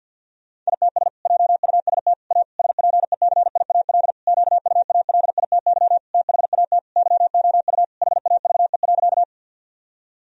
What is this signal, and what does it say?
Morse code, heard as a keyed tone, about 50 wpm, 700 Hz. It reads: ITS JUST A SPECIAL QUALITY THAT YOU HAVE.
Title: Sentences from Top 1000 Words - Review Part 4 - 50wpm